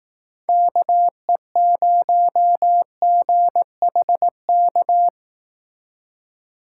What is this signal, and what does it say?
Morse code, heard as a keyed tone, about 18 wpm, 700 Hz. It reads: KE0GHK
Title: Speed Racing - USA Call Signs 15wpm